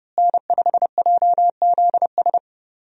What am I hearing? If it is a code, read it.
Morse code, heard as a keyed tone, about 30 wpm, 700 Hz. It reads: N5JZS
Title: USA Call Signs - 5 Characters - Repeated 2x - 30wpm